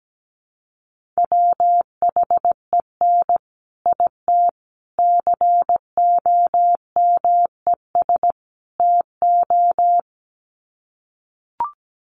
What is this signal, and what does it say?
Morse code, heard as a keyed tone, about 17 wpm, 700 Hz. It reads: WHEN IT COMES TO
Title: Sets of 4 Words 17wpm